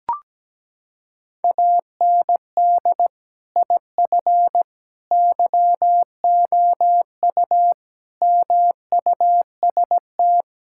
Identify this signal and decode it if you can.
Morse code, heard as a keyed tone, about 17 wpm, 700 Hz. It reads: AND IF YOU MUST
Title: Sentences from Top 1600 Words 17wpm